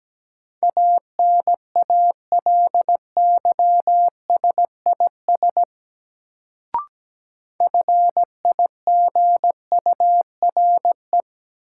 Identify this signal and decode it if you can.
Morse code, heard as a keyed tone, about 17 wpm, 700 Hz. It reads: ANALYSIS FIGURE
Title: Sentences from Top 1400 Words - New Word Drill Alt - 17wpm